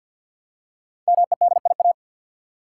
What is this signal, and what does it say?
Morse code, heard as a keyed tone, about 53 wpm, 700 Hz. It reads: MEDIA
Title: Five Letter Words - 50wpm